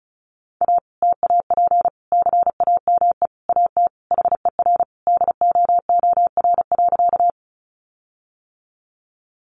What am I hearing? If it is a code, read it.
Morse code, heard as a keyed tone, about 34 wpm, 700 Hz. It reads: A TAP CAME AT HER DOOR.
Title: Sentences from Top 1400 Words 35wpm